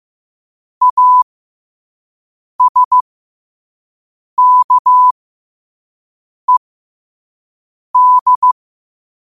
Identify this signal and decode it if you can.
Morse code, heard as a keyed tone, about 15 wpm, spaced as for 5 wpm, 1000 Hz. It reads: ASKED
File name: qotdfiles / 5wpm